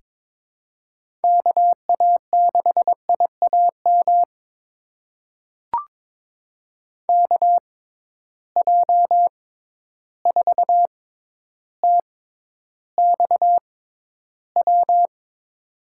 Morse code, heard as a keyed tone, about 22 wpm, spaced as for 11 wpm, 700 Hz. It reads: KA6IAM KJ4TXW
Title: USA Call Signs - 6 Characters - 6x Character Spacing - 22wpm